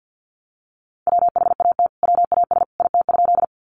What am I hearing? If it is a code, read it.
Morse code, heard as a keyed tone, about 50 wpm, 700 Hz. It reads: WHAT WAS IT?
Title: Sentences from Top 100 Words 50wpm